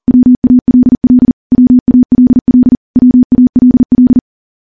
Morse code, heard as a keyed tone, about 40 wpm, 250 Hz. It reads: WALL WALL WALL
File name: morse-training-40WPM